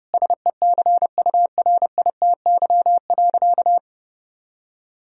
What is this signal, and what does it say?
Morse code, heard as a keyed tone, about 32 wpm, 700 Hz. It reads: SECURITY.
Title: Sentences from Top 1000 Words - Review Part 4 - 30wpm